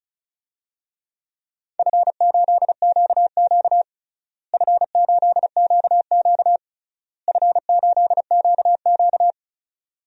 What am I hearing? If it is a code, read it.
Morse code, heard as a keyed tone, about 36 wpm, 700 Hz. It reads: F8QQ F8QQ F8QQ
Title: DXCC-100 Call Signs - 4 Characters - Repeated 3x - 35wpm